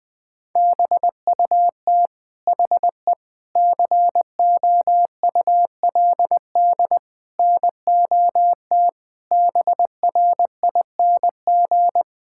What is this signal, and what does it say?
Morse code, heard as a keyed tone, about 20 wpm, 700 Hz. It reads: BUT HE COULD NOT BRING
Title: Sentences from Top 1600 Words 20wpm